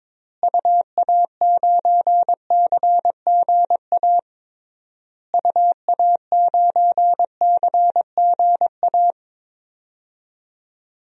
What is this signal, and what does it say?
Morse code, heard as a keyed tone, about 22 wpm, 700 Hz. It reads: UA9CGA UA9CGA
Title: DXCC-100 Call Signs - 6 Characters - Repeated 3x - 22wpm